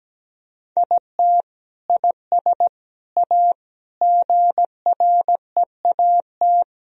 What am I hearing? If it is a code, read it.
Morse code, heard as a keyed tone, about 17 wpm, 700 Hz. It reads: IT IS A GREAT
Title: Sentences from Top 1000 Words - Review Part 1 - 17wpm